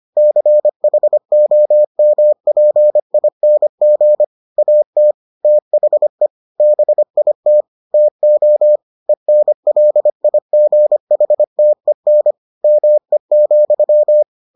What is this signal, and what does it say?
Morse code, heard as a keyed tone, about 25 wpm, 600 Hz. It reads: CHOMPING AT THE BIT TO ENLIGHTEN ME,